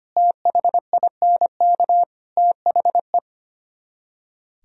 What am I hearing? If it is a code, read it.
Morse code, heard as a keyed tone, about 33 wpm, 700 Hz. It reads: THINK THE